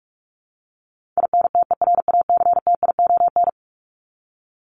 Morse code, heard as a keyed tone, about 44 wpm, 700 Hz. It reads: INTERACTION
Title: Top 1251-1500 Words 45wpm